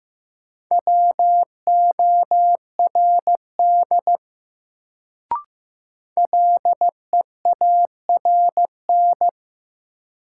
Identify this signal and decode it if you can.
Morse code, heard as a keyed tone, about 15 wpm, 700 Hz. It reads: WORD LEARN
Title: Sentences from Top 300 Words - New Word Drill - 15wpm